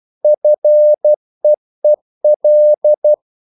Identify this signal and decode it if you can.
Morse code, heard as a keyed tone, about 12 wpm, 600 Hz. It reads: FEEL
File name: CW / Practice / EnglishQuotations0312